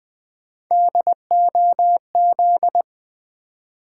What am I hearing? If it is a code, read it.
Morse code, heard as a keyed tone, about 20 wpm, 700 Hz. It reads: DOZ
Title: Build A Word - 1 Letter At A Time - 5-Letter Words - 20wpm